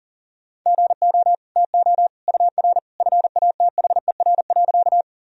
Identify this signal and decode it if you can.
Morse code, heard as a keyed tone, about 40 wpm, 700 Hz. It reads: GO TO UR FATHER.
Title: Sentences from Top 300 Words and Abbreviations 40wpm